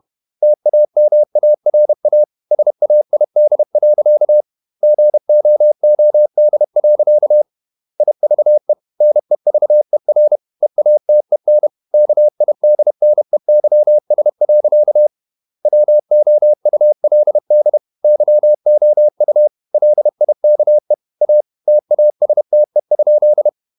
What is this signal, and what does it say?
Morse code, heard as a keyed tone, about 31 wpm, 600 Hz. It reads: TAMARA SAID. GOOD. IVE NEVER EATEN KIDNEYS. WOULD YOU LIKE A TASTE?
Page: Never 31wpm | CW med Gnister